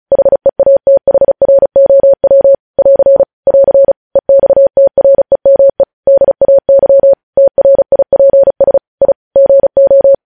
Morse code, heard as a keyed tone, about 36 wpm, 550 Hz. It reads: HEATHROW <AR> <AR> EXTREME DAY TRIPS I GO
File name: podcast / News headlines-1x-35x35